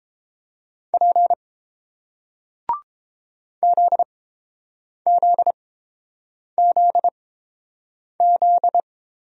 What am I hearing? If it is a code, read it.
Morse code, heard as a keyed tone, about 30 wpm, spaced as for 7 wpm, 690 Hz. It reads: P ZZZZ